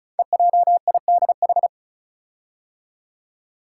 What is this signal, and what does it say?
Morse code, heard as a keyed tone, about 36 wpm, 690 Hz. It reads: EJIDH
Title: Code Groups - 5 Letters - 35wpm